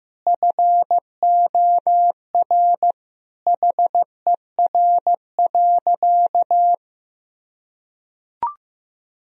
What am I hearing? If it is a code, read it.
Morse code, heard as a keyed tone, about 16 wpm, 700 Hz. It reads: FOR HER.
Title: Sentences from Top 1000 Words - Review Part 3 - 15wpm